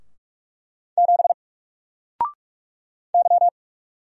Morse code, heard as a keyed tone, about 46 wpm, 700 Hz. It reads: 7 Y
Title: Single Letter-Number 45wpm